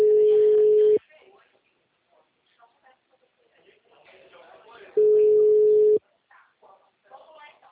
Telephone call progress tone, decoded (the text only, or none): ringback tone